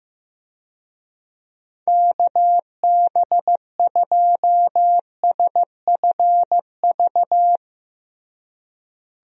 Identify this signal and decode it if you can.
Morse code, heard as a keyed tone, about 16 wpm, 700 Hz. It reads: KB2SFV
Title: USA Call Signs - 6 Characters - Repeated 2x - 15wpm